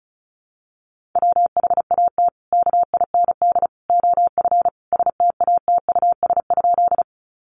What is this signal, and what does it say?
Morse code, heard as a keyed tone, about 34 wpm, 700 Hz. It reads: WHAT KIND OF STATUS?
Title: Sentences from Top 1400 Words 35wpm